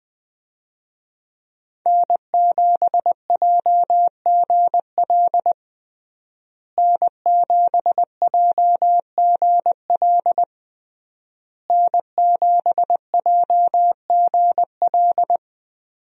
Morse code, heard as a keyed tone, about 20 wpm, 700 Hz. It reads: N7JGL N7JGL N7JGL
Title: USA Call Signs - 5 Characters - Repeated 3x - 20wpm